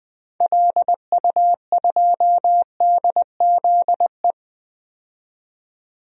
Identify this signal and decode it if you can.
Morse code, heard as a keyed tone, about 20 wpm, 700 Hz. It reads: LU2DZE
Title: DXCC-100 Call Signs - 6 Characters - 20wpm